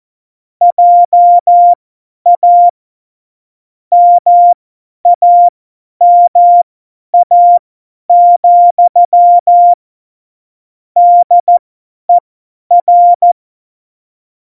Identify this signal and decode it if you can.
Morse code, heard as a keyed tone, about 14 wpm, spaced as for 10 wpm, 700 Hz. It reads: JA MAMA, DER
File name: Morsekurs / Witze / Witz 26. Dezember 2017, 50 F70